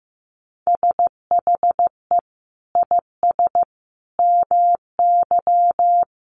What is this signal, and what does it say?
Morse code, heard as a keyed tone, about 14 wpm, 700 Hz. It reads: SHE IS MY
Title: Sentences from Top 300 Words 15wpm